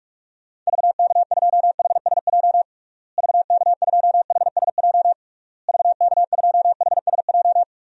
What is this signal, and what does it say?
Morse code, heard as a keyed tone, about 46 wpm, 700 Hz. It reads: VK2HSJ VK2HSJ VK2HSJ
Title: DXCC-100 Call Signs - 6 Characters - Repeated 3x - 45wpm